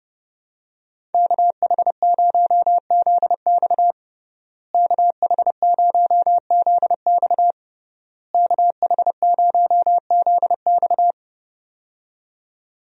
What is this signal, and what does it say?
Morse code, heard as a keyed tone, about 30 wpm, 700 Hz. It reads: KH0ZX KH0ZX KH0ZX
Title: USA Call Signs - 5 Characters - Repeated 3x - 30wpm